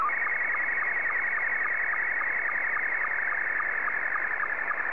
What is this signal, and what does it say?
SSTV (slow-scan television), Scottie 2.
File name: sstv / S2-VK3DEK